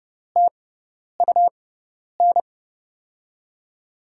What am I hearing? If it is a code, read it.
Morse code, heard as a keyed tone, about 30 wpm, spaced as for 10 wpm, 700 Hz. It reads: TUN
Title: USA Call Signs - 6 Characters - 6x Character Spacing - 30wpm